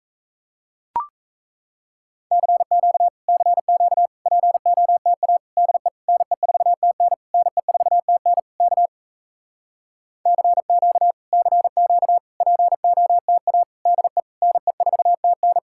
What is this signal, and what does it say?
Morse code, heard as a keyed tone, about 38 wpm, 700 Hz. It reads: CQ CQ POTA DE NE4TN NE4TN K CQ CQ POTA DE NE4TN